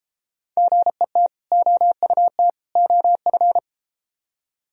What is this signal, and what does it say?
Morse code, heard as a keyed tone, about 33 wpm, 700 Hz. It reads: GET OUT OF